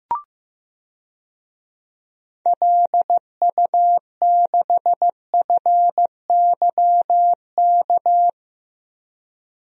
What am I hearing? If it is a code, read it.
Morse code, heard as a keyed tone, about 15 wpm, 700 Hz. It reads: LU6FYK
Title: DXCC-100 Call Signs - 6 Characters - Repeated 3x - 15wpm